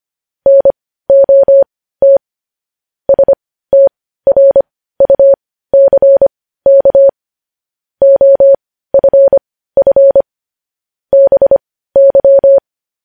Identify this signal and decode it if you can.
Morse code, heard as a keyed tone, about 25 wpm, spaced as for 15 wpm, 550 Hz. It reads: NOT STRUCK OFF BY
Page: News Headlines in Morse Code at 15 WPM